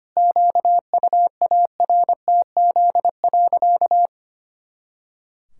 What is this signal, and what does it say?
Morse code, heard as a keyed tone, about 25 wpm, 700 Hz. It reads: QUARTZ.
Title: Pangram Challenge v2 - 25wpm